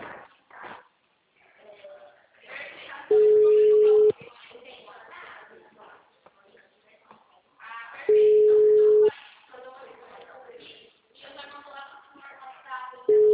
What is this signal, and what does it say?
Ringback tone (Europe, 425 Hz), with speech.
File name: out- -2001- - - .14322